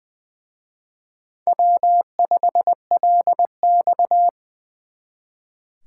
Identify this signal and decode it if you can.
Morse code, heard as a keyed tone, about 24 wpm, 700 Hz. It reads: W5LX
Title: Speed Racing - USA 1x2 Call Signs 20wpm